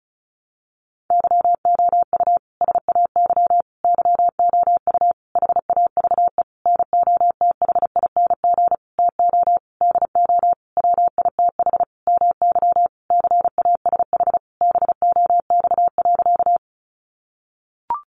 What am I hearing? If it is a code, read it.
Morse code, heard as a keyed tone, about 34 wpm, 700 Hz. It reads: YOU SAY YOU HAVE NOTHING TO DO WITH MY CASH BOX.
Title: Sentences from Top 700 Words 35wpm